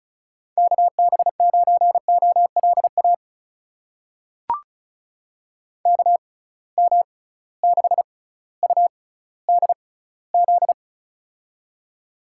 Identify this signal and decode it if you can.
Morse code, heard as a keyed tone, about 36 wpm, spaced as for 18 wpm, 700 Hz. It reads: KB9OLA KM6UDZ
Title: USA Call Signs - 6 Characters - 6x Character Spacing - 35wpm